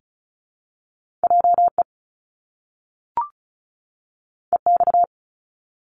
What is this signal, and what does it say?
Morse code, heard as a keyed tone, about 36 wpm, 700 Hz. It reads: JE EX
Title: Top Two Letter Combinations in Words 35wpm